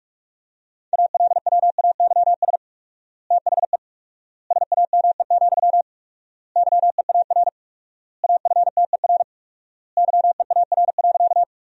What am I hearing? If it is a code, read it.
Morse code, heard as a keyed tone, about 46 wpm, 700 Hz. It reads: ALWAYS THE SAME, YEAR AFTER YEAR.
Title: 4x Word-Spacing - Sentences from Top 300 Words - 45wpm